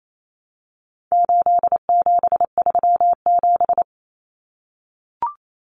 Morse code, heard as a keyed tone, about 28 wpm, 700 Hz. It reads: 8737
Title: Common Cut Numbers - 4-Digit 28wpm